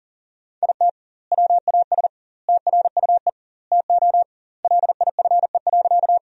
Morse code, heard as a keyed tone, about 40 wpm, 700 Hz. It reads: IT WAS TRUE TO LIFE.
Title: 2x Word-Spacing - Sentences from Top 400 Words - 40wpm